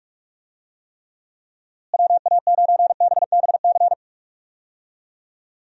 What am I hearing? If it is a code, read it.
Morse code, heard as a keyed tone, about 46 wpm, 700 Hz. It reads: WA9BBC
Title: USA Call Signs - 6 Characters - Repeated 2x - 45wpm